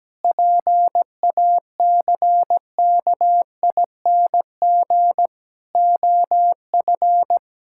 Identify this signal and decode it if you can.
Morse code, heard as a keyed tone, about 17 wpm, 700 Hz. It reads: PACKING OF
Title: Pangram Challenge v2 - 17wpm